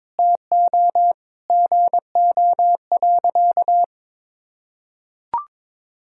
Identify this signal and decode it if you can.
Morse code, heard as a keyed tone, about 22 wpm, 700 Hz. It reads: TO GO.
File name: practice / 22wpm / Sentences with 4-Word Phrases - 22wpm